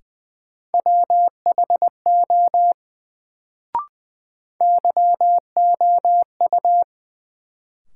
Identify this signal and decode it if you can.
Morse code, heard as a keyed tone, about 21 wpm, 700 Hz. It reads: WHO YOU
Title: Sets of 2 Words - New Word Drill - 20wpm